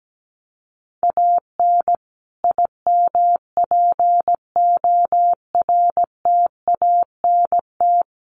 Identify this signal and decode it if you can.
Morse code, heard as a keyed tone, about 17 wpm, 700 Hz. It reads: AN IMPORTANT
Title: Sets of 5 Words Encore2 17wpm